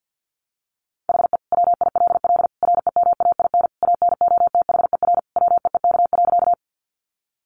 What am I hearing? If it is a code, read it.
Morse code, heard as a keyed tone, about 50 wpm, 700 Hz. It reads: HE WILL REMAIN ANOTHER WEEK.
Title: Sentences from Top 1200 Words 50wpm